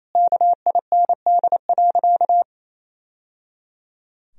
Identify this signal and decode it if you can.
Morse code, heard as a keyed tone, about 28 wpm, 700 Hz. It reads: KIND.
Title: Sentences from Top 1000 Words - Review Part 3 - 28wpm